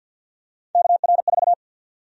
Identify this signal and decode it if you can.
Morse code, heard as a keyed tone, about 52 wpm, 700 Hz. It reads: KR4
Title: USA Call Sign Prefixes - 50wpm